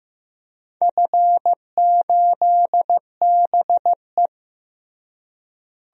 Morse code, heard as a keyed tone, about 15 wpm, 700 Hz. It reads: F8BE
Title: DXCC-100 Call Signs - 4 Characters - Repeated 2x - 15wpm